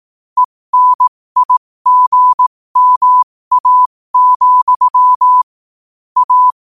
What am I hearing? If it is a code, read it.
Morse code, heard as a keyed tone, about 18 wpm, spaced as for 15 wpm, 1000 Hz. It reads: ENIGMA, A
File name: qotdfiles / 15wpm